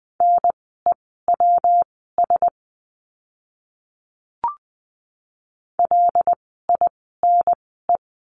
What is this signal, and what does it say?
Morse code, heard as a keyed tone, about 20 wpm, spaced as for 14 wpm, 700 Hz. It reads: NEWS LINE
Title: 115 Lesson 19f - Letter w - Words 20wpm